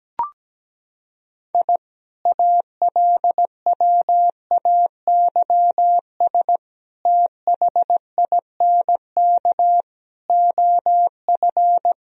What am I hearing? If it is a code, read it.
Morse code, heard as a keyed tone, about 17 wpm, 700 Hz. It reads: I ALWAYS THINK OF
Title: Sentences from Top 1000 Words - Review Part 3 - 17wpm